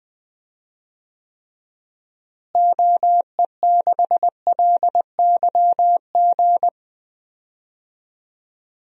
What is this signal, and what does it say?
Morse code, heard as a keyed tone, about 20 wpm, 700 Hz. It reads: OE6LYG
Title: DXCC-100 Call Signs - 6 Characters - Repeated 2x - 20wpm